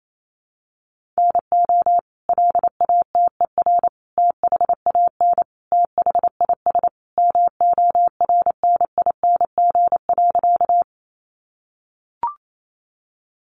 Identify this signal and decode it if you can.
Morse code, heard as a keyed tone, about 28 wpm, 700 Hz. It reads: NO LATER THAN THIS MORNING.
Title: Sentences from Top 700 Words 28wpm